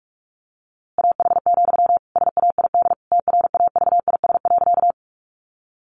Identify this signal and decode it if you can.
Morse code, heard as a keyed tone, about 44 wpm, 700 Hz. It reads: AH, SAID TRAVIS.
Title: A Sound Of Thunder 45wpm